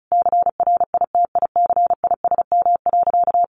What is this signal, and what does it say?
Morse code, heard as a keyed tone, about 34 wpm, 700 Hz. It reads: CRITICISM.
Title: Sentences from Top 1300 Words 35wpm